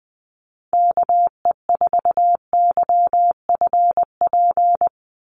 Morse code, heard as a keyed tone, about 20 wpm, 700 Hz. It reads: KE4YFP
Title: USA Call Signs 20wpm